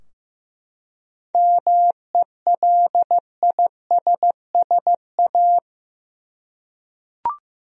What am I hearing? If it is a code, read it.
Morse code, heard as a keyed tone, about 15 wpm, 700 Hz. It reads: MELISSA
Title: USA Common Names - 15wpm